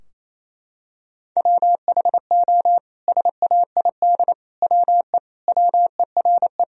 Morse code, heard as a keyed tone, about 27 wpm, 700 Hz. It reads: WHO SAID WE WERE
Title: Sentences from Top 1000 Words - Review Part 4 - 28wpm